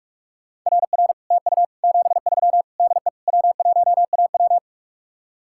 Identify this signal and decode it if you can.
Morse code, heard as a keyed tone, about 48 wpm, 700 Hz. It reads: RR TU 73 DE W1AW